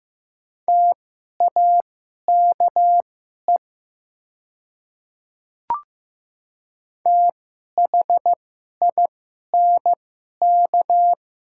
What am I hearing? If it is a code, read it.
Morse code, heard as a keyed tone, about 15 wpm, spaced as for 11 wpm, 700 Hz. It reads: TAKE THINK
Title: 2x Character Spacing - Top 200 Words - 15wpm